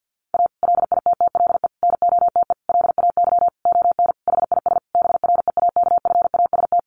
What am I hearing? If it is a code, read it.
Morse code, heard as a keyed tone, about 50 wpm, 700 Hz. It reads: A LITTLE NOTE LAY ON HIS BREAKFAST
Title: Sentences from Top 1500 Words 50wpm